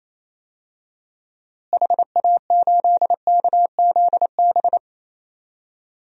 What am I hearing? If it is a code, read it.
Morse code, heard as a keyed tone, about 28 wpm, 700 Hz. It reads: HA8KZB
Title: DXCC-100 Call Signs - 6 Characters - Repeated 3x - 28wpm